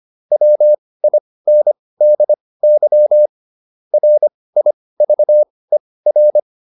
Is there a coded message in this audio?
Yes: Morse code from 0.3 to 6.4 s, heard as a keyed tone, about 25 wpm, spaced as for 18 wpm, 600 Hz. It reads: WINDY RIVER